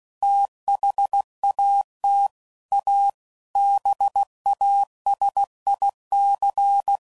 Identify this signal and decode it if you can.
Morse code, heard as a keyed tone, about 16 wpm, 800 Hz. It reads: THAT A BASIC
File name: CW Code Groups @15WPM / Technical Subjects / PwrSupply